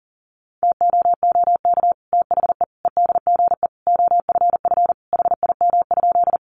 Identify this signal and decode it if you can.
Morse code, heard as a keyed tone, about 40 wpm, 700 Hz. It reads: TOOK THE EDGE OFF HIM?
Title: Sentences from Top 1300 Words 40wpm